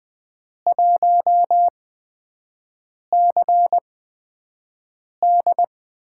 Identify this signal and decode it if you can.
Morse code, heard as a keyed tone, about 20 wpm, spaced as for 5 wpm, 700 Hz. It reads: 1CD
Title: USA Call Signs - 8x Character Spacing - 20wpm